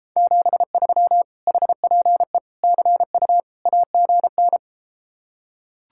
Morse code, heard as a keyed tone, about 33 wpm, 700 Hz. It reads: 73 HPE CU AGN